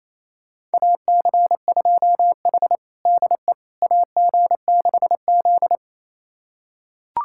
Morse code, heard as a keyed tone, about 28 wpm, 700 Hz. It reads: AC2H DE AG6Z
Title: USA Callsign DE Callsign - 4 Characters - 28wpm